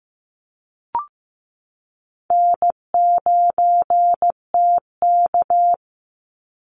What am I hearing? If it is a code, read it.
Morse code, heard as a keyed tone, about 15 wpm, 700 Hz. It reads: N9TK
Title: Easy Call Signs Repeated 3x - 15wpm